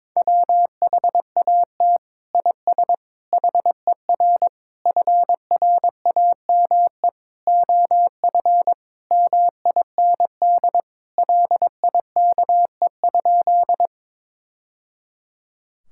Morse code, heard as a keyed tone, about 22 wpm, 700 Hz. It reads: WHAT IS HER FRAME OF MIND LIKE?
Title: Sentences from Top 1000 Words - Review Part 6 - 22wpm